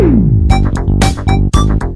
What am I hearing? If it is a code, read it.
There is no code.